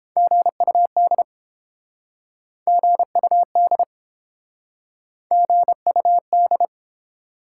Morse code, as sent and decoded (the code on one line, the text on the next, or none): --. ..- -.. / --. ..- -.. / --. ..- -..
GUD GUD GUD